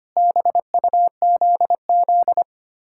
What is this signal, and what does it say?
Morse code, heard as a keyed tone, about 32 wpm, 700 Hz. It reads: BUZZ